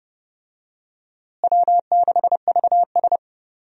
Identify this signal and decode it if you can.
Morse code, heard as a keyed tone, about 30 wpm, 700 Hz. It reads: W6VS
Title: USA Call Signs - 4 Characters - Repeated 3x - 30wpm